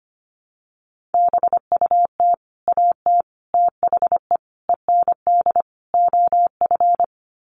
Morse code, heard as a keyed tone, about 25 wpm, 700 Hz. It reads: BUT AT THE END OF
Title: Sets of 5 Words Encore2 25wpm